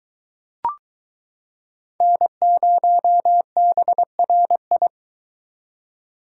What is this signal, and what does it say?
Morse code, heard as a keyed tone, about 22 wpm, 700 Hz. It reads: N0BRI